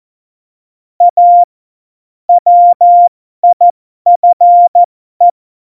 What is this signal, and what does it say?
Morse code, heard as a keyed tone, about 14 wpm, spaced as for 12 wpm, 700 Hz. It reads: A WIFE
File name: Morsekurs / Witze